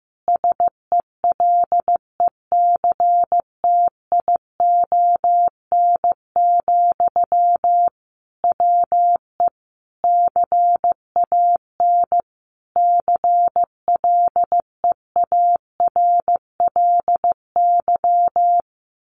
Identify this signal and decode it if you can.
Morse code, heard as a keyed tone, about 15 wpm, 700 Hz. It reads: SELECTION, WE CAN CLEARLY
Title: Sentences from Top 1400 Words 15wpm